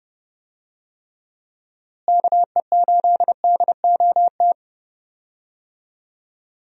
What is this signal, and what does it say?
Morse code, heard as a keyed tone, about 31 wpm, 700 Hz. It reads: KE8DOT
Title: USA Call Signs - 6 Characters - Repeated 3x - 30wpm